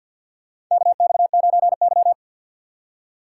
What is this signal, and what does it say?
Morse code, heard as a keyed tone, about 52 wpm, 690 Hz. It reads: KX9Y
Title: USA Call Signs - 4 Characters - 6x Character Spacing - 50wpm